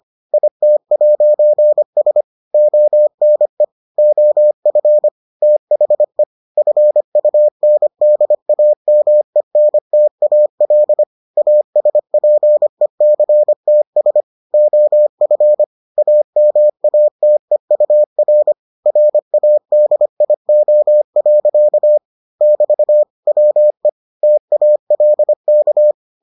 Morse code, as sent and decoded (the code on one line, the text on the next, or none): .. - .----. ... / --- -. / --- ..-. / - .... / ..-. ..- -. -.. .- -- . -. - .- .-.. / .- ... .--. -.-. - ... / --- ..-. / .- -- .- - . ..- .-. / .-. .- -.. .. --- .-.-.- / -...- / .-- . / - .- .-.. -.-
IT'S ONE OF THE FUNDAMENTAL ASPECTS OF AMATEUR RADIO. <BT> WE TALK